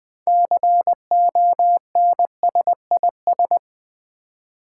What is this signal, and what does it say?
Morse code, heard as a keyed tone, about 20 wpm, 700 Hz. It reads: CONSIS
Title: Build A Word - 6 Letters At A Time - 12-Letter Words - 20wpm